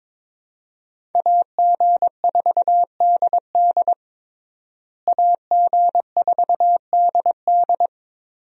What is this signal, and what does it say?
Morse code, heard as a keyed tone, about 22 wpm, 700 Hz. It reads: AG4DD AG4DD
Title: USA Call Signs - 5 Characters - Repeated 3x - 22wpm